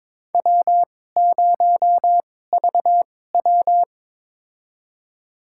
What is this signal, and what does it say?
Morse code, heard as a keyed tone, about 22 wpm, spaced as for 16 wpm, 700 Hz. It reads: W0VW
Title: USA Call Signs - 4 Characters - 2x Character Spacing - 22wpm